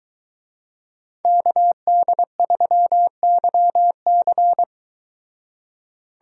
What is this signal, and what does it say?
Morse code, heard as a keyed tone, about 23 wpm, 700 Hz. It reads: KD3YC